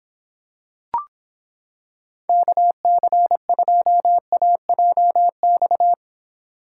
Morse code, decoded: KC2AJX

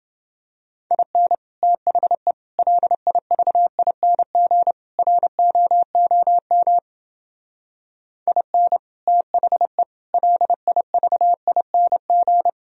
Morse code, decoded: IN THE LIVING ROOM IN THE LIVING

